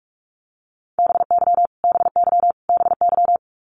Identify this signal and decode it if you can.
Morse code, heard as a keyed tone, about 44 wpm, 700 Hz. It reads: BY BY BY
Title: Top 200 Words Repeated 45wpm